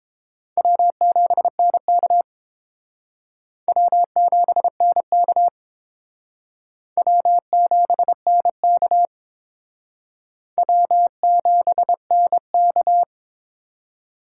Morse code, heard as a keyed tone, about 30 wpm, 700 Hz. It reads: W7NK W7NK W7NK W7NK